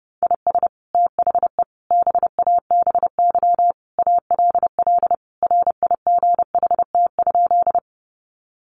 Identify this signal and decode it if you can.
Morse code, heard as a keyed tone, about 28 wpm, 700 Hz. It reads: IS THE BABY ALL RIGHT?
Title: Sentences from Top 1300 Words 30wpm